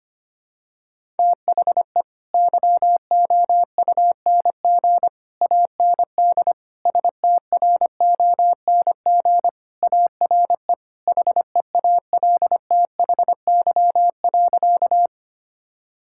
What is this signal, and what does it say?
Morse code, heard as a keyed tone, about 25 wpm, 700 Hz. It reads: THE YOUNG AND STRONG ARE HEALTHY.
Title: Sentences from Top 1000 Words - Review Part 5 - 25wpm